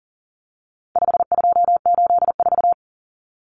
Morse code, heard as a keyed tone, about 40 wpm, 700 Hz. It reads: 5284
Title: Common Cut Numbers - 4-Digit 40wpm